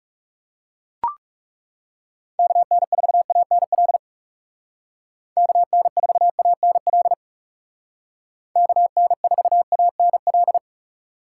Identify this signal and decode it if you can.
Morse code, heard as a keyed tone, about 39 wpm, 700 Hz. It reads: KN4ANL KN4ANL KN4ANL